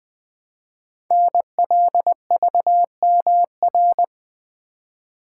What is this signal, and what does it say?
Morse code, heard as a keyed tone, about 20 wpm, 700 Hz. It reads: NLVMR
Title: Code Groups - 5 Letters - 20wpm